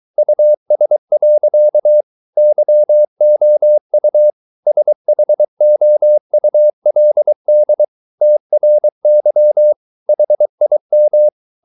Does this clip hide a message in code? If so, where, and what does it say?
Yes: Morse code from 0.2 to 11.3 s, heard as a keyed tone, about 23 wpm, 600 Hz. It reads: US. YOU SHOULD TRY HIM